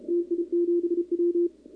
Morse code, heard as a keyed tone, about 31 wpm, 340 Hz, with radio noise. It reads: TI7W
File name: contests / 2019-cw-wpx-kt5j-k5pi- -2330 0-11